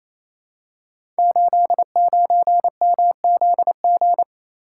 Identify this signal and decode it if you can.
Morse code, heard as a keyed tone, about 28 wpm, 700 Hz. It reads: 89MZG
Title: Code Groups - 5 Letter:Numbers - 28wpm